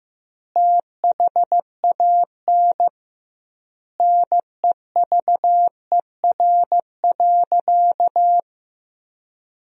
Morse code, heard as a keyed tone, about 16 wpm, 700 Hz. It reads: THAN NEVER.
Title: 2x Word-Spacing - Sentences from Top 400 Words - 15wpm